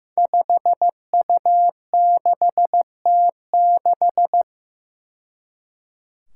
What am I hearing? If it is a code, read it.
Morse code, heard as a keyed tone, about 15 wpm, 700 Hz. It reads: 5U6T6
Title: Five Challenging Letter-Number Combinations 15wpm